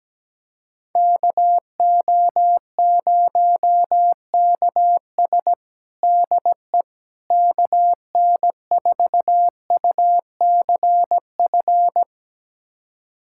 Morse code, heard as a keyed tone, about 17 wpm, 700 Hz. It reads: KO0KS DE KN4UCF
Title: USA Callsign DE Callsign 17wpm